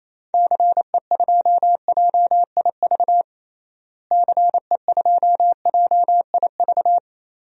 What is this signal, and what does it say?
Morse code, heard as a keyed tone, about 28 wpm, 700 Hz. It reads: CE2JIV CE2JIV
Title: DXCC-100 Call Signs - 6 Characters - Repeated 3x - 28wpm